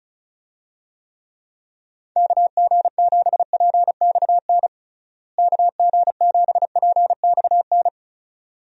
Morse code, heard as a keyed tone, about 36 wpm, 700 Hz. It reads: KG7PXN KG7PXN
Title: USA Call Signs - 6 Characters - Repeated 2x - 35wpm